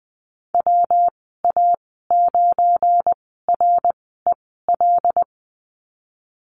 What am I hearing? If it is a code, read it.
Morse code, heard as a keyed tone, about 20 wpm, spaced as for 15 wpm, 700 Hz. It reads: WA9REL
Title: 186 Lesson 30g - Letter b - Call Signs 20wpm